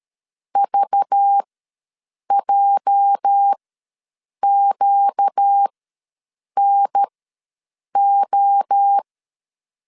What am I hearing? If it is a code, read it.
Morse code, heard as a keyed tone, about 13 wpm, spaced as for 7 wpm, 800 Hz. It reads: VJQNO
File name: LEZIONI / 8 YJ45 / 8 YJ ws 12wpm 3342